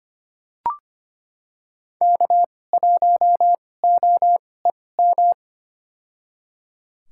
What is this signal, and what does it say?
Morse code, heard as a keyed tone, about 25 wpm, spaced as for 18 wpm, 700 Hz. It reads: K1OEM
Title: USA Call Signs - 5 Characters - 2x Character Spacing - 25wpm